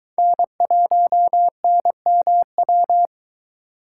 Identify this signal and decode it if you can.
Morse code, heard as a keyed tone, about 23 wpm, 700 Hz. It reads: N1NMW